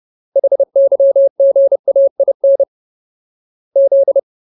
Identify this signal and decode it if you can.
Morse code, heard as a keyed tone, about 30 wpm, 550 Hz. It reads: HYGAIN Z
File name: Ultimate Word List / 30WPM Ultimate